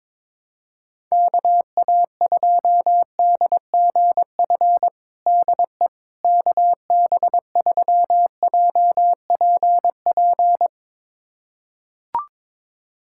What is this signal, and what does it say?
Morse code, heard as a keyed tone, about 22 wpm, 700 Hz. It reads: KA2DGF DE KB3JPP
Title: USA Callsign DE Callsign - 6 Characters - 22wpm